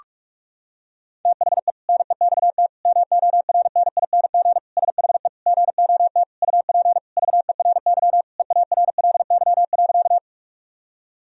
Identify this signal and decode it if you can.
Morse code, heard as a keyed tone, about 46 wpm, 700 Hz. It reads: THE NEXT MORNING SHE GOT UP VERY EARLY.
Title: Sentences from Top 1000 Words - Review Part 2 - 45wpm